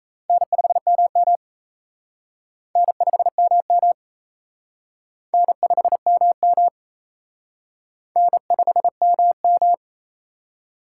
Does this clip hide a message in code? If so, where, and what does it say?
Yes: Morse code from 0.3 to 9.7 s, heard as a keyed tone, about 36 wpm, 700 Hz. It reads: N5MM N5MM N5MM N5MM